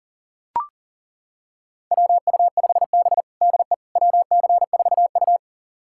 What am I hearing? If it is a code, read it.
Morse code, heard as a keyed tone, about 40 wpm, 700 Hz. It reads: WU5B DE WC4U
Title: USA Callsign DE Callsign - 4 Characters - 40wpm